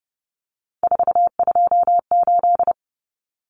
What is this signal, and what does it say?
Morse code, heard as a keyed tone, about 30 wpm, 700 Hz. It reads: 428
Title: Common Cut Numbers - 3-Digit 30wpm